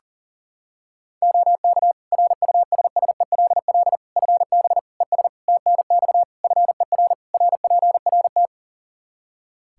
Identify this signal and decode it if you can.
Morse code, heard as a keyed tone, about 40 wpm, 700 Hz. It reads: OK RUSSELL FB ES TNX FER RPRT
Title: Practice QSO - Exchange 2 - Info - 40wpm